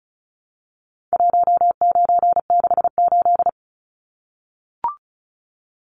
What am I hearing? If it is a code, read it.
Morse code, heard as a keyed tone, about 34 wpm, 700 Hz. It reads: 1968
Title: Numbers - 4-Digit 35wpm